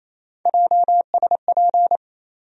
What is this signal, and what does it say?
Morse code, heard as a keyed tone, about 28 wpm, 700 Hz. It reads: JSP
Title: USA Call Sign Postfixes - 3 Letters 28wpm